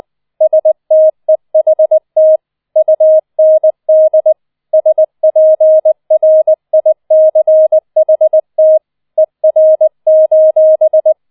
Morse code, heard as a keyed tone, about 20 wpm, 630 Hz. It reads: STEHT UND SPRICHT ER: